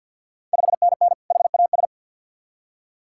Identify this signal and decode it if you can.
Morse code, heard as a keyed tone, about 46 wpm, 690 Hz. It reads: 5NN HAS
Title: Ontario QSO Party Exchange 50wpm